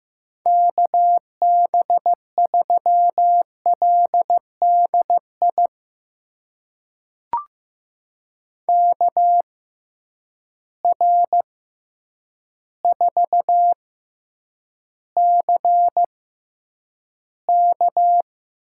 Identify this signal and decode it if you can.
Morse code, heard as a keyed tone, about 15 wpm, spaced as for 8 wpm, 700 Hz. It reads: KB3LDI KR4CK
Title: USA Call Signs - 6 Characters - 6x Character Spacing - 15wpm